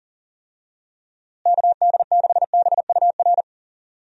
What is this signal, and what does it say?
Morse code, heard as a keyed tone, about 40 wpm, 700 Hz. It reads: KD6BUR
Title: USA Call Signs - 6 Characters - Repeated 2x - 40wpm